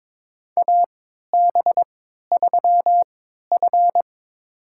Morse code, heard as a keyed tone, about 22 wpm, spaced as for 13 wpm, 700 Hz. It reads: AB3F
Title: USA Call Signs - 4 Characters - 3x Character Spacing - 22wpm